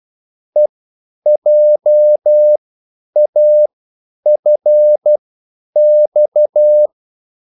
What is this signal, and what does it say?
Morse code, heard as a keyed tone, about 12 wpm, spaced as for 9 wpm, 600 Hz. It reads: EJAFX